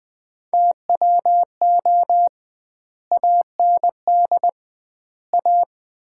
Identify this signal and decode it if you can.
Morse code, heard as a keyed tone, about 21 wpm, 700 Hz. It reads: TWO AND A
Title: 2x Word-Spacing - Sets of 4 Words - 20wpm